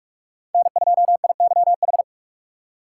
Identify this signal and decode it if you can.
Morse code, heard as a keyed tone, about 46 wpm, 700 Hz. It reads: N2IYH
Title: USA Call Signs - 5 Characters - 2x Character Spacing - 45wpm